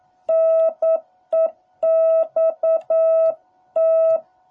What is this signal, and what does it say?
Morse code, heard as a keyed tone, about 9 wpm, 650 Hz. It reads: NEXT